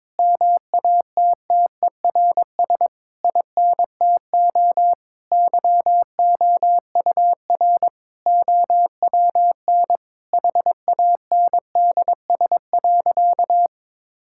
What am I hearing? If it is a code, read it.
Morse code, heard as a keyed tone, about 22 wpm, 700 Hz. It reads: MATTERS INTO YOUR OWN HANDS.
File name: Sentences with 7-Word Phrases - 22wpm